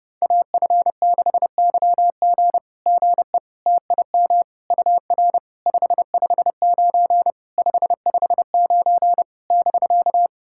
Morse code, heard as a keyed tone, about 30 wpm, 690 Hz. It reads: AF6YG GE TIM UR 559 559 <BK>
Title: SOTA QSO Practice 30wpm